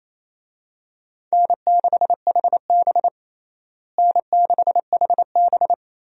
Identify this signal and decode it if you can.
Morse code, heard as a keyed tone, about 27 wpm, 700 Hz. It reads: N6HB N6HB
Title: USA Call Signs - 4 Characters - Repeated 2x - 28wpm